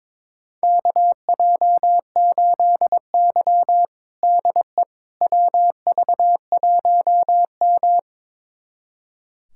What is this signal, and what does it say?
Morse code, heard as a keyed tone, about 21 wpm, 700 Hz. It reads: KJ8Y DE WV1M
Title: USA Callsign DE Callsign - 4 Characters - 22wpm